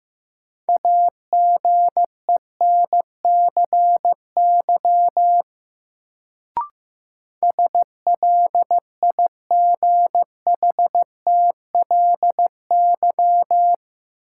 Morse code, heard as a keyed tone, about 15 wpm, 700 Hz. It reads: AGENCY SLIGHTLY
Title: Sentences from Top 1300 Words - New Word Drill - 15wpm